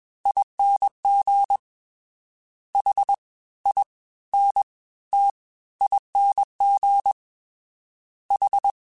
Morse code, heard as a keyed tone, about 21 wpm, spaced as for 15 wpm, 800 Hz. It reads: ING HINTING H